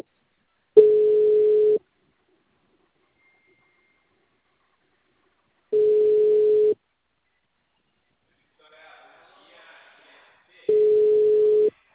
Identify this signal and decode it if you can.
Ringback tone (Europe, 425 Hz), with speech.
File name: out- -2014- - - .25368